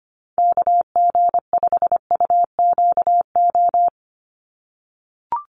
Morse code, heard as a keyed tone, about 25 wpm, 700 Hz. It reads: KG5UQO
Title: USA Call Signs 25wpm